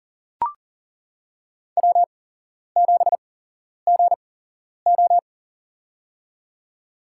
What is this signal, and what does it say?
Morse code, heard as a keyed tone, about 40 wpm, spaced as for 11 wpm, 700 Hz. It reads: W7GO
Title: USA Call Signs - 8x Character Spacing - 40wpm